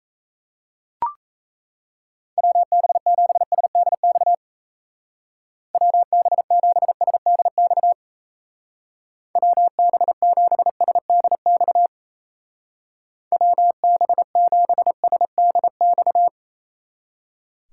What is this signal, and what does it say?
Morse code, heard as a keyed tone, about 34 wpm, 700 Hz. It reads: WB7SDX WB7SDX WB7SDX WB7SDX
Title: Speed Racing - USA 2x3 Call Signs 28wpm